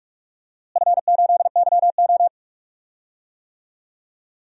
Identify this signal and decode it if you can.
Morse code, heard as a keyed tone, about 46 wpm, 700 Hz. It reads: F8YO
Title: DXCC-100 Call Signs - 4 Characters - Repeated 2x - 45wpm